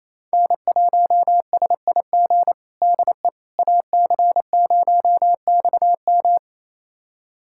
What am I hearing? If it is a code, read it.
Morse code, heard as a keyed tone, about 27 wpm, 700 Hz. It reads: N1SIG DE AC0XM
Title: USA Callsign DE Callsign - 5 Characters - 28wpm